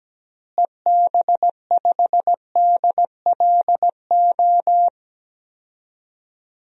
Morse code, heard as a keyed tone, about 17 wpm, 700 Hz. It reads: EB5DLO
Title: DXCC-100 Call Signs - 6 Characters - Repeated 2x - 17wpm